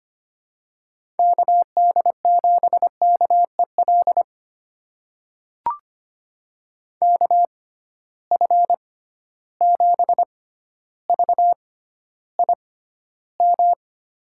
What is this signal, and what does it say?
Morse code, heard as a keyed tone, about 25 wpm, spaced as for 13 wpm, 700 Hz. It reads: KD7KEL KF7VIM
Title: USA Call Signs - 6 Characters - 6x Character Spacing - 25wpm